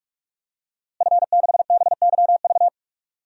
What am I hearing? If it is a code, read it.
Morse code, heard as a keyed tone, about 44 wpm, 700 Hz. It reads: F6BYV
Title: DXCC-100 Call Signs - 5 Characters - Repeated 3x - 45wpm